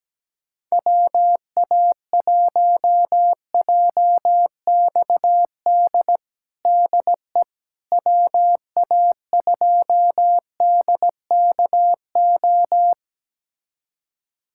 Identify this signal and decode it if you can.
Morse code, heard as a keyed tone, about 17 wpm, 700 Hz. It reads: WA1JXD DE WA2DKO
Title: USA Callsign DE Callsign - 6 Characters - 17wpm